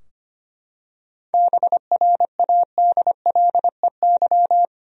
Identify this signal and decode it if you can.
Morse code, heard as a keyed tone, about 25 wpm, 700 Hz. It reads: BRADLEY
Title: USA Common Names - 25wpm